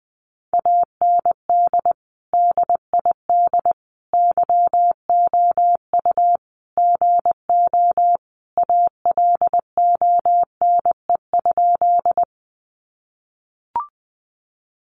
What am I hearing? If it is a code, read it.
Morse code, heard as a keyed tone, about 20 wpm, 700 Hz. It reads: AND DID YOU GO ALONE?
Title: Sentences from Top 1000 Words 20wpm